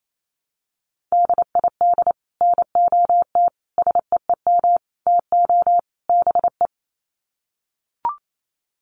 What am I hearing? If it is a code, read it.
Morse code, heard as a keyed tone, about 28 wpm, 700 Hz. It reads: DID NOT SEEM TO BE
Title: Sets of 5 Words Encore 28wpm